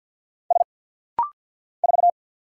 Morse code, heard as a keyed tone, about 50 wpm, 700 Hz. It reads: S 4
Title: Single Letter-Number - Rapid-Fire 50wpm